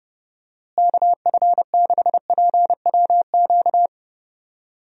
Morse code, heard as a keyed tone, about 30 wpm, 700 Hz. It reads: KF6PWQ
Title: USA Call Signs - 6 Characters - Repeated 3x - 30wpm